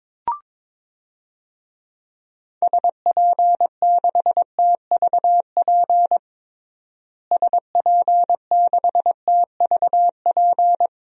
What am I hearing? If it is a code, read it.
Morse code, heard as a keyed tone, about 22 wpm, 700 Hz. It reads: SP6TVP SP6TVP
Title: DXCC-100 Call Signs - 6 Characters - Repeated 2x - 22wpm